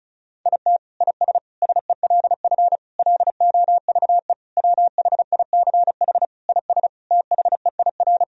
Morse code, heard as a keyed tone, about 34 wpm, 700 Hz. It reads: IT IS SELF LOVE WHICH IS THEIR
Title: Sentences from Top 1000 Words - Review Part 1 - 35wpm